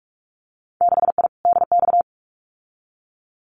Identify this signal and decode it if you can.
Morse code, heard as a keyed tone, about 46 wpm, 700 Hz. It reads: 6I DX
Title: Winter Field Day Exchanges Encore 45wpm